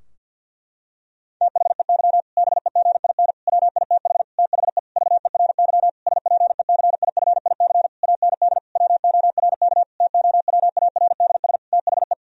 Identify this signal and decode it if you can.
Morse code, heard as a keyed tone, about 50 wpm, 700 Hz. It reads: THEY BEGIN WITH THE VERY SPECIFIC AND WORK TOWARDS THE
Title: Sentences from Top 1000 Words - Review Part 3 - 50wpm